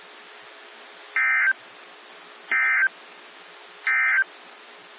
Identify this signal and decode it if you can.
Emergency Alert System (SAME) tones, heard as harsh buzzing bursts. It is heard from 1.0 to 4.3 s.
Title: NOAA Weather Radio - WXJ73